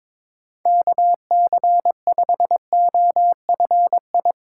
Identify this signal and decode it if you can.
Morse code, heard as a keyed tone, about 22 wpm, 700 Hz. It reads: KC5OFI
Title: USA Call Signs - 6 Characters - 4x Character Spacing - 22wpm